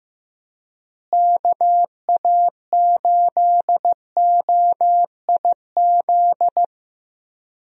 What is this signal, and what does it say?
Morse code, heard as a keyed tone, about 15 wpm, 700 Hz. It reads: KA8OIZ
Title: USA Call Signs - 6 Characters - Repeated 3x - 15wpm